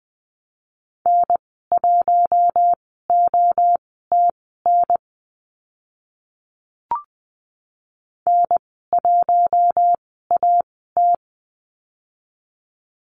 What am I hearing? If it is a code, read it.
Morse code, heard as a keyed tone, about 20 wpm, spaced as for 14 wpm, 700 Hz. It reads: N1OTN N1AT
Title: 048 Lesson 9g - Number 4 - Call Signs 20wpm